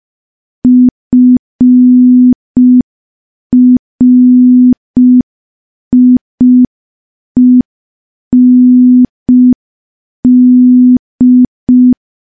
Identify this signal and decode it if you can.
Morse code, heard as a keyed tone, about 5 wpm, 250 Hz. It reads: FRIEND
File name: morse / morse-training-05WPM